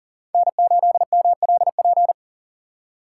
Morse code, heard as a keyed tone, about 40 wpm, 700 Hz. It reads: N8MLP
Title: USA Call Signs - 3x Character Spacing - 40wpm